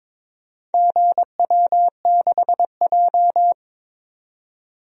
Morse code, heard as a keyed tone, about 22 wpm, 700 Hz. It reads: GW6J
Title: DXCC-100 Call Signs - 4 Characters - 22wpm